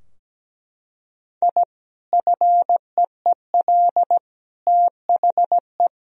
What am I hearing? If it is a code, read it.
Morse code, heard as a keyed tone, about 17 wpm, 700 Hz. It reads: I FEEL THE
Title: Sentences from Top 1000 Words - Review Part 3 - 17wpm